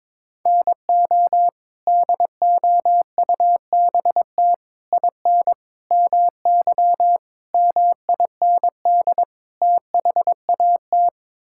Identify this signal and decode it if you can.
Morse code, heard as a keyed tone, about 22 wpm, 700 Hz. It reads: NO DOUBT IN MY MIND THAT
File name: practice / 22wpm / Sets of 6 Words - 22wpm